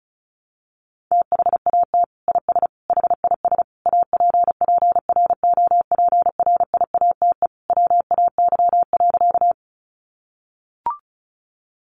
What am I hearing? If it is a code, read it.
Morse code, heard as a keyed tone, about 34 wpm, 700 Hz. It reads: THAT IS HIS APPROPRIATE WAY.
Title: Sentences from Top 900 Words 35wpm